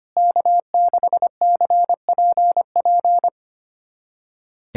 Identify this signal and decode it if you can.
Morse code, heard as a keyed tone, about 25 wpm, 700 Hz. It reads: K6CPP
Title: USA Callsign DE Callsign - 5 Characters - 25wpm